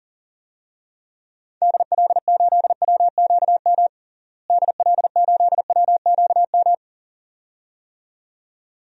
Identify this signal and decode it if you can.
Morse code, heard as a keyed tone, about 40 wpm, 700 Hz. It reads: DL8WQM DL8WQM
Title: DXCC-100 Call Signs - 6 Characters - Repeated 2x - 40wpm